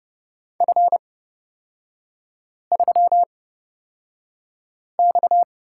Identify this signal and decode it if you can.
Morse code, heard as a keyed tone, about 30 wpm, spaced as for 5 wpm, 700 Hz. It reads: F3X
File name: Lesson 37e - Letter x - Three Characters 30wpm